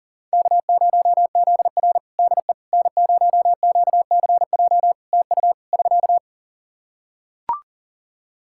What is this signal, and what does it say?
Morse code, heard as a keyed tone, about 40 wpm, 700 Hz. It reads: K0ZR DE N0QCJ TU <SK>
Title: Practice QSO - Exchange 4 - Ending - 40wpm